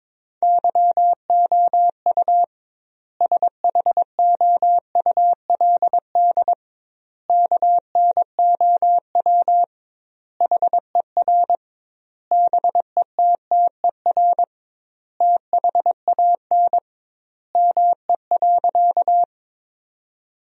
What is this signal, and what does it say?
Morse code, heard as a keyed tone, about 21 wpm, 700 Hz. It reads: YOU SHOULD KNOW HER BETTER THAN ME.
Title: 2x Word-Spacing - Sentences from Top 400 Words - 22wpm